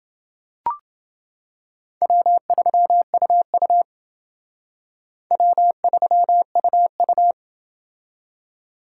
Morse code, heard as a keyed tone, about 28 wpm, 700 Hz. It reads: W3UU W3UU